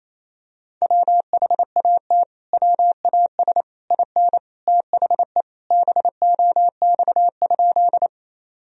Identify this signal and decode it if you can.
Morse code, heard as a keyed tone, about 28 wpm, 700 Hz. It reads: WHAT WAS IN THE BOX?
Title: Sentences from Top 1000 Words - Review Part 2 - 28wpm